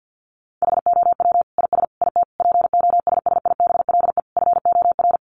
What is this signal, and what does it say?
Morse code, heard as a keyed tone, about 50 wpm, 700 Hz. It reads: HOW IS IT POSSIBLE FOR
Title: Sentences from Top 900 Words 50wpm